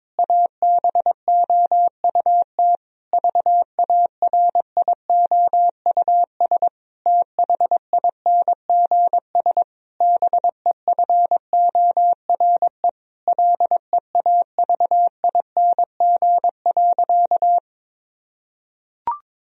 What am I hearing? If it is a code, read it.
Morse code, heard as a keyed tone, about 22 wpm, 700 Hz. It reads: ABOUT VARIOUS THINGS BEFORE LEAVING.
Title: Sentences from Top 1000 Words - Review Part 4 - 22wpm